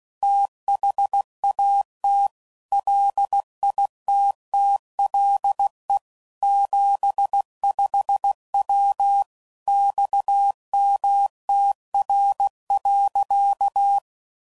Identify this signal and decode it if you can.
Morse code, heard as a keyed tone, about 16 wpm, 800 Hz. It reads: THAT LITTLE 75W XMTR.
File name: CW Code Groups @15WPM / QSOs / WA7QOX